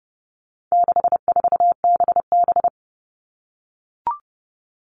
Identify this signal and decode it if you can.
Morse code, heard as a keyed tone, about 30 wpm, 700 Hz. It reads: 64BB
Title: All Cut Numbers - 4-Digit 30wpm